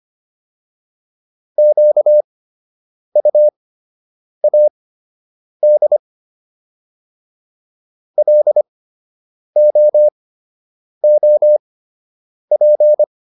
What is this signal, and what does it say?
Morse code, heard as a keyed tone, about 25 wpm, spaced as for 8 wpm, 600 Hz. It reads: QUAD LOOP